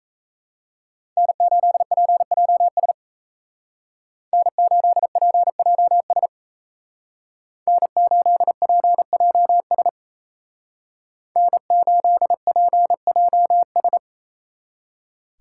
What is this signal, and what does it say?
Morse code, heard as a keyed tone, about 36 wpm, 700 Hz. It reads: N8PJS N8PJS N8PJS N8PJS